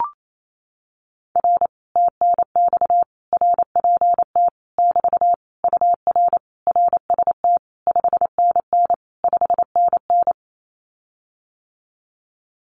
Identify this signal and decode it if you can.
Morse code, heard as a keyed tone, about 28 wpm, 700 Hz. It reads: R TNX RPT <BT> UR RST 5NN 5NN
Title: Common QSO Phrases Encore 28wpm